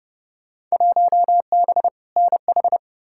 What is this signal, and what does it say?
Morse code, heard as a keyed tone, about 30 wpm, 700 Hz. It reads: 1B NH
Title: Field Day Exchanges Encore - 30wpm